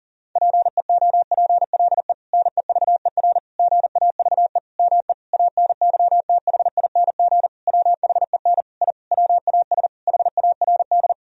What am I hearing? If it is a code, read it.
Morse code, heard as a keyed tone, about 40 wpm, 700 Hz. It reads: PEOPLE NEVER GAVE ME ANYTHING WHEN I WAS HARD
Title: Sentences from Top 1000 Words - Review Part 1 - 40wpm